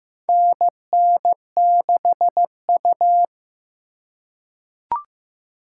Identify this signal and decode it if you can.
Morse code, heard as a keyed tone, about 15 wpm, 700 Hz. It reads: NN6U
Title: USA Call Signs - 4 Characters - 3x Character Spacing - 15wpm